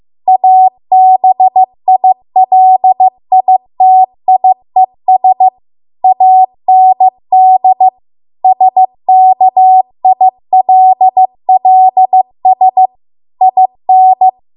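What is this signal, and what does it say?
Morse code, heard as a keyed tone, about 15 wpm, 750 Hz. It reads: ABILITIES AND SKILLS IN
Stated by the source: Morse code practice at 15 wpm from the national amateur radio association's code archive.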